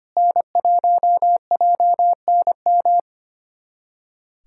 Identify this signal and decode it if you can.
Morse code, heard as a keyed tone, about 25 wpm, 700 Hz. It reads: N1JNM
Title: USA Callsign DE Callsign - 5 Characters - 25wpm